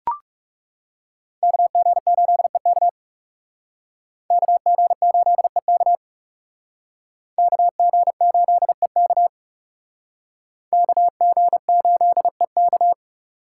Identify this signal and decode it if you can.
Morse code, heard as a keyed tone, about 39 wpm, 700 Hz. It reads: KG8EK KG8EK KG8EK KG8EK